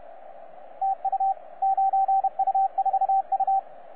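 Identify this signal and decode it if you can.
Morse code, heard as a keyed tone, about 31 wpm, 780 Hz, with radio noise. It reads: TU 9U4U
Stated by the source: 9U4U on 15 metre CW, DX station recording by DX World.net.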